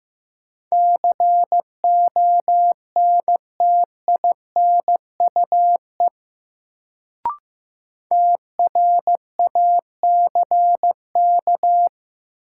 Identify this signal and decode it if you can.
Morse code, heard as a keyed tone, about 15 wpm, 700 Hz. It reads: CONTINUE TRACK
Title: Sentences from Top 1100 Words - New Word Drill Alt - 15wpm